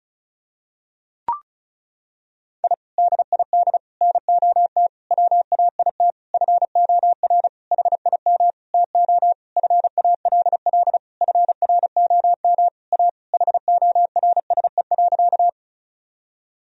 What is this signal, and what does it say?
Morse code, heard as a keyed tone, about 34 wpm, 700 Hz. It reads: I DID NOT WAIT FOR HIM TO FALL FROM A HORSE.
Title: Sentences from Top 1000 Words - Review Part 2 - 35wpm